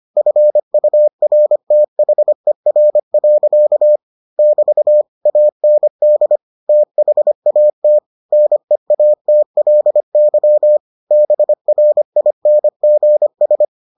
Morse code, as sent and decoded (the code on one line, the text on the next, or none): ..-. ..- .-. - .... .-. .-.-.- / -...- / .- -. -.. / - .... .- - / -. .- - .-.. -.-- / -... .-. .. -. --. ...
FURTHER. <BT> AND THAT NEATLY BRINGS